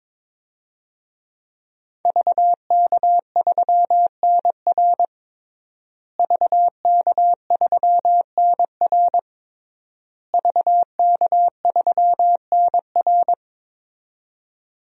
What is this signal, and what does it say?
Morse code, heard as a keyed tone, about 21 wpm, 700 Hz. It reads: VK3NR VK3NR VK3NR
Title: DXCC-100 Call Signs - 5 Characters - Repeated 3x - 22wpm